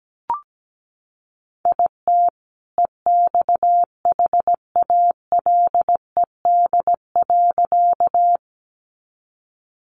Morse code, heard as a keyed tone, about 17 wpm, 700 Hz. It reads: IT EXHALED.
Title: A Sound Of Thunder 17wpm